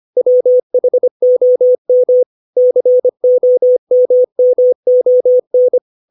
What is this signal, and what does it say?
Morse code, heard as a keyed tone, about 25 wpm, 500 Hz. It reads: WHOM COMMON
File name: 632 words / 632 Words 25WPM